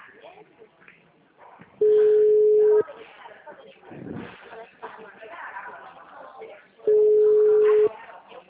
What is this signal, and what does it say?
Ringback tone (Europe, 425 Hz), with speech.
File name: out- -2016- - - .1298